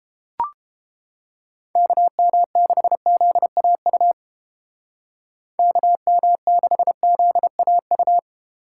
Morse code, heard as a keyed tone, about 32 wpm, 700 Hz. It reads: KM6ZAU KM6ZAU